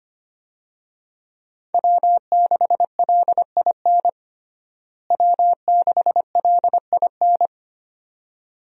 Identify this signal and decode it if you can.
Morse code, heard as a keyed tone, about 25 wpm, 700 Hz. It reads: W6LIN W6LIN
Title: USA Call Signs - 5 Characters - Repeated 2x - 25wpm